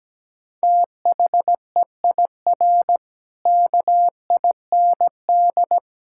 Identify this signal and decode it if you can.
Morse code, heard as a keyed tone, about 17 wpm, 700 Hz. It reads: THEIR KIND
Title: Pangram Challenge v2 - 17wpm